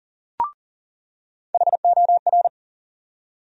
Morse code, heard as a keyed tone, about 40 wpm, 700 Hz. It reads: HOR
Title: USA Call Sign Postfixes - 3 Letters 40wpm